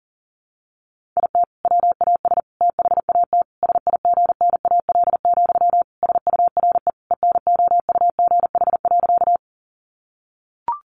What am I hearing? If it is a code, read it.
Morse code, heard as a keyed tone, about 40 wpm, 700 Hz. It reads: IT WAS THAT SIGNAL, SURE ENOUGH.
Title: Sentences from Top 1400 Words 40wpm